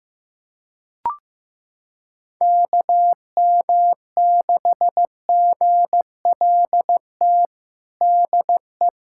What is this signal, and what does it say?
Morse code, heard as a keyed tone, about 16 wpm, 700 Hz. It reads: KM6GLT DE
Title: USA Callsign DE Callsign 15wpm